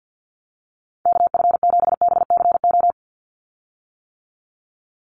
Morse code, heard as a keyed tone, about 50 wpm, 700 Hz. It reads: KF7BCO
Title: USA Call Signs 50wpm